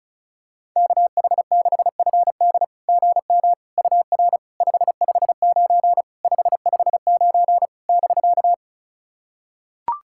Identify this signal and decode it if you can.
Morse code, heard as a keyed tone, about 36 wpm, 690 Hz. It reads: KH6FD GM UR 559 559 <BK>
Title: SOTA QSO Practice 35wpm